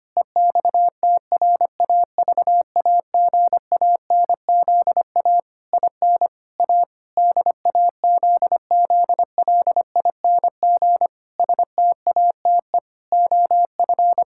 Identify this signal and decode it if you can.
Morse code, heard as a keyed tone, about 25 wpm, 700 Hz. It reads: EXTRAVAGANZA IN A DAZZLING STATE OF
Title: Pangram Challenge v2 - 25wpm